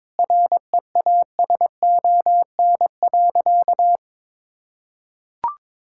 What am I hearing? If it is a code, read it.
Morse code, heard as a keyed tone, about 22 wpm, 700 Hz. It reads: REASON.
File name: practice / 22wpm / Sentences with 8-Word Phrases - 22wpm